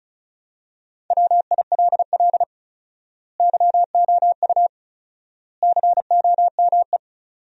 Morse code, heard as a keyed tone, about 36 wpm, 700 Hz. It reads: WILL YOU COME
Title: 4x Word-Spacing - Sentences from Top 100 Words - 35wpm